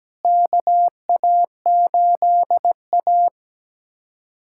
Morse code, heard as a keyed tone, about 17 wpm, 700 Hz. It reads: KA8A
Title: USA Callsign DE Callsign - 4 Characters - 17wpm